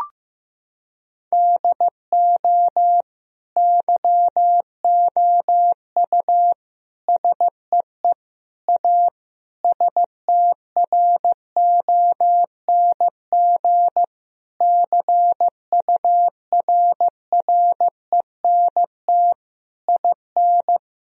Morse code, heard as a keyed tone, about 14 wpm, 700 Hz. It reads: DO YOU SEE A STRONG CURRENT IN
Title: Sentences from Top 1000 Words - Review Part 4 - 15wpm